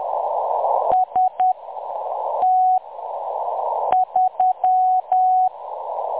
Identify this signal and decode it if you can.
Morse code, heard as a keyed tone, about 10 wpm, spaced as for 6 wpm, 740 Hz. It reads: ST3
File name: M23-ST3- -0858utc-10184khz